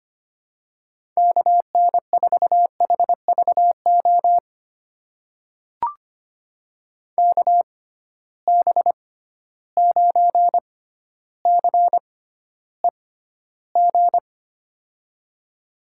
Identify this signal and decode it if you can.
Morse code, heard as a keyed tone, about 25 wpm, spaced as for 13 wpm, 700 Hz. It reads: KN4HVO KB9CEG
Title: USA Call Signs - 6 Characters - 6x Character Spacing - 25wpm